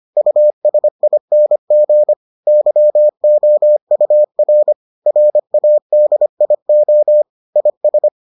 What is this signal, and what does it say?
Morse code, heard as a keyed tone, about 25 wpm, 600 Hz. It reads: USING YOUR RADIO IS